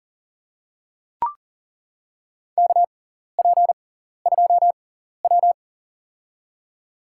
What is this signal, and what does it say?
Morse code, heard as a keyed tone, about 40 wpm, spaced as for 14 wpm, 700 Hz. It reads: KP2W
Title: USA Call Signs - 6x Character Spacing - 40wpm